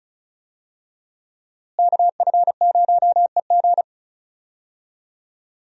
Morse code, heard as a keyed tone, about 34 wpm, 700 Hz. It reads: KF0EG
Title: USA Call Signs - 5 Characters - Repeated 2x - 35wpm